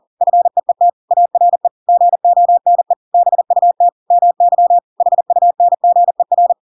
Morse code, heard as a keyed tone, about 40 wpm, 700 Hz. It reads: FEET ARE GONE BUT MY HUNGER
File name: Pinocchio / Pinocchio ch7 40 40 ik1hge